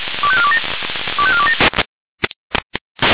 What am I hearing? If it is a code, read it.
Five-tone selective call (ZVEI), heard as quick runs of five tones, tones of about 70 ms. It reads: 25427 25427